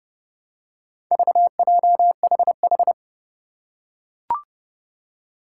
Morse code, heard as a keyed tone, about 30 wpm, 700 Hz. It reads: VJHH
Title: Four Challenging Letter-Number Combinations 30wpm